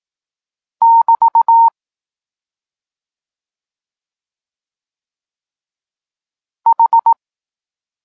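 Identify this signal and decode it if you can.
Morse code, heard as a keyed tone, about 18 wpm, 920 Hz. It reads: <BT> H
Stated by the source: Morse code lesson for learning the letters K and X, from the SP9MOA Niepołomicki Klub Krótkofalowców course.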